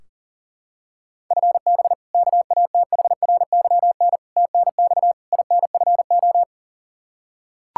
Morse code, heard as a keyed tone, about 40 wpm, 700 Hz. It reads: FB KATHRYN TNX INFO
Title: Practice QSO - Exchange 3 - More Info - 40wpm